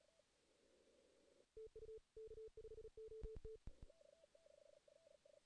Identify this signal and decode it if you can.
Morse code, heard as a keyed tone, about 35 wpm, 440 Hz. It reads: TU K5OT
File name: 2016 JAN CW NAQP / 2016 Jan naqp k5ot - 0-2